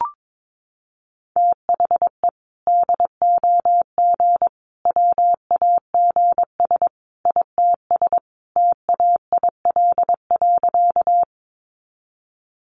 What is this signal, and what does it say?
Morse code, heard as a keyed tone, about 22 wpm, 700 Hz. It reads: THE DOG WAGS ITS TAIL.
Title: Easy Phrases 22wpm